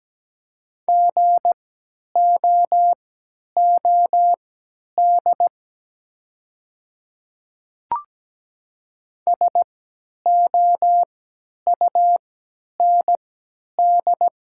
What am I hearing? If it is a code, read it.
Morse code, heard as a keyed tone, about 17 wpm, spaced as for 10 wpm, 700 Hz. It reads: GOOD SOUND
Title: 3x Character Spacing - Top 200 Words - 17wpm